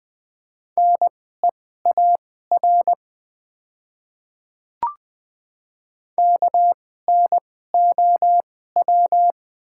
Morse code, heard as a keyed tone, about 20 wpm, spaced as for 14 wpm, 700 Hz. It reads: NEAR KNOW
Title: 2x Character Spacing - Top 200 Words - 20wpm